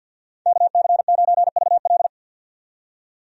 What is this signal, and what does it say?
Morse code, heard as a keyed tone, about 50 wpm, 700 Hz. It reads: KC9VL
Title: USA Call Signs - 5 Characters - 50wpm